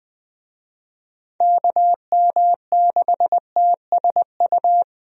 Morse code, heard as a keyed tone, about 21 wpm, 700 Hz. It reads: KM6TSU
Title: USA Call Signs - 6 Characters - 20wpm